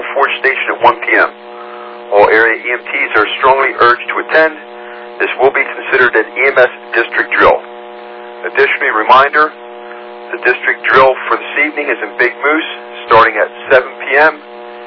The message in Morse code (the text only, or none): none